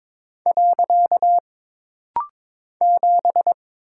Morse code, heard as a keyed tone, about 22 wpm, 700 Hz. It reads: . 7
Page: This rapid-fire timing is geared to push your ICR (Instant Character Recognition) skills.